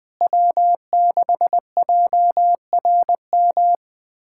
Morse code, heard as a keyed tone, about 21 wpm, 700 Hz. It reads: W6JRM
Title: USA Callsign DE Callsign - 5 Characters - 20wpm